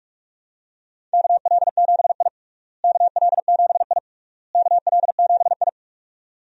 Morse code, heard as a keyed tone, about 46 wpm, 700 Hz. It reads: KL7I KL7I KL7I
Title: DXCC-100 Call Signs - 4 Characters - Repeated 3x - 45wpm